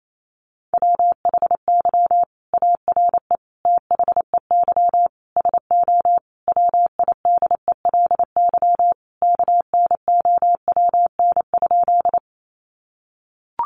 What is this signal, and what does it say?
Morse code, heard as a keyed tone, about 28 wpm, 700 Hz. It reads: WHY ARE THEY SO WIDELY KNOWN?
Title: Sentences from Top 1200 Words 28wpm